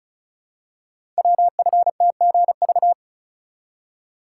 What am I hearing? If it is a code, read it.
Morse code, heard as a keyed tone, about 36 wpm, 690 Hz. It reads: WFTGV
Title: Code Groups - 5 Letters - 35wpm